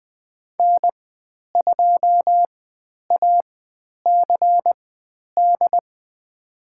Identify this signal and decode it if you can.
Morse code, heard as a keyed tone, about 20 wpm, spaced as for 10 wpm, 700 Hz. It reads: N2ACD
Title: 095 Lesson 16g - Letter c - Call Signs 20wpm